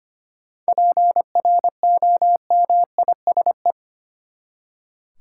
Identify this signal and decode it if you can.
Morse code, heard as a keyed tone, about 25 wpm, 700 Hz. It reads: PROMISE
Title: Seven Letter Words - 25wpm